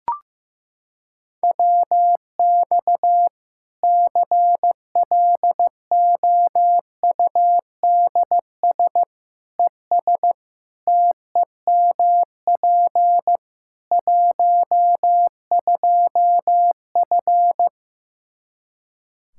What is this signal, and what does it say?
Morse code, heard as a keyed tone, about 15 wpm, 700 Hz. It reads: WX CLOUDS ES TEMP 12F
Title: K1USN SST QSO Practice 15wpm